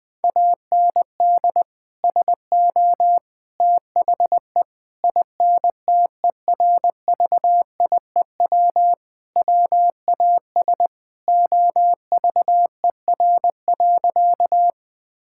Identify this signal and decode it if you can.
Morse code, heard as a keyed tone, about 21 wpm, 700 Hz. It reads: AND SO THE INTERVIEW WAS OVER.
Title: Sentences from Top 1600 Words 20wpm